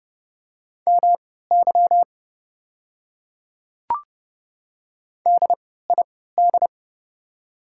Morse code, heard as a keyed tone, about 30 wpm, spaced as for 17 wpm, 700 Hz. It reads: MY DID
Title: 3x Character Spacing - Top 200 Words - 30wpm